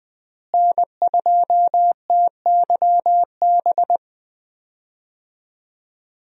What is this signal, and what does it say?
Morse code, heard as a keyed tone, about 20 wpm, 700 Hz. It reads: N2TYB
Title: USA Call Signs - 5 Characters - Repeated 3x - 20wpm